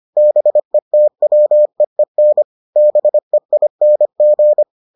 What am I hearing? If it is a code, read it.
Morse code, heard as a keyed tone, about 25 wpm, 600 Hz. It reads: BETWEEN BEING